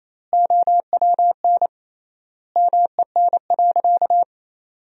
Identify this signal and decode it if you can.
Morse code, heard as a keyed tone, about 28 wpm, 700 Hz. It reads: OWN MEN.
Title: 3x Word-Spacing - Sentences from Top 300 Words - 28wpm